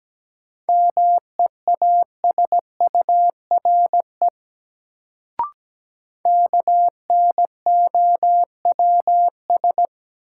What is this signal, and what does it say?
Morse code, heard as a keyed tone, about 17 wpm, 700 Hz. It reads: MEASURE KNOWS
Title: Sentences from Top 1000 Words - Review Part 1 - New Word Drill - 17wpm